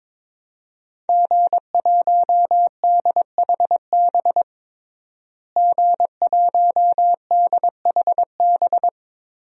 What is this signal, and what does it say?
Morse code, heard as a keyed tone, about 22 wpm, 700 Hz. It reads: G1DHB G1DHB
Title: DXCC-100 Call Signs - 5 Characters - Repeated 3x - 22wpm